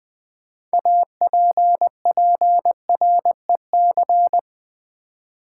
Morse code, heard as a keyed tone, about 20 wpm, 700 Hz. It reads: APPREC
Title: Build A Word - 6 Letters At A Time - 12-Letter Words - 20wpm